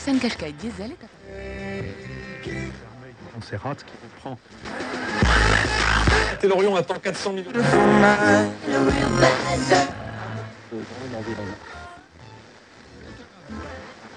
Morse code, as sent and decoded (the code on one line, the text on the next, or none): none